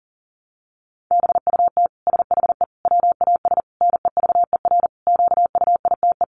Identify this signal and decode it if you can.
Morse code, heard as a keyed tone, about 37 wpm, 700 Hz. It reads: BUT SHE WAS NEVER QUITE
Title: Sentences from Top 700 Words 40wpm